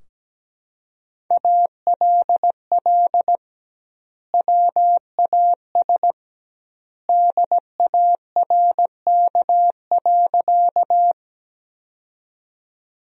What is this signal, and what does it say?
Morse code, heard as a keyed tone, about 17 wpm, 700 Hz. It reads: ALL WAS DARK.
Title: 2x Word-Spacing - Sentences from Top 500 Words - 17wpm